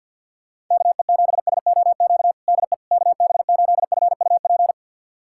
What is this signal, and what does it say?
Morse code, heard as a keyed tone, about 52 wpm, 700 Hz. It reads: KE7SOQ DE KB8FUP
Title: USA Callsign DE Callsign 50wpm